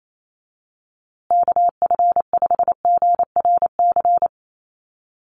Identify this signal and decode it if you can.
Morse code, heard as a keyed tone, about 27 wpm, 700 Hz. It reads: KF5GRC
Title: USA Call Signs Repeated 2x 28wpm